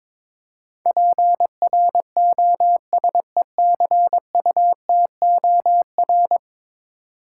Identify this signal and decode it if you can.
Morse code, heard as a keyed tone, about 22 wpm, 700 Hz. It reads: PROSECUTOR
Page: morse-code-ninja-practice-22wpm